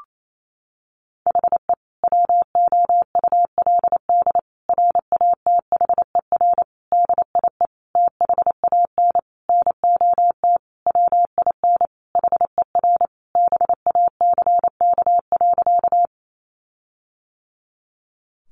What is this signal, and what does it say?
Morse code, heard as a keyed tone, about 28 wpm, 700 Hz. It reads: HE WOULD RATHER DIE THAN NOT WIN HER BACK.
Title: Sentences from Top 1500 Words 28wpm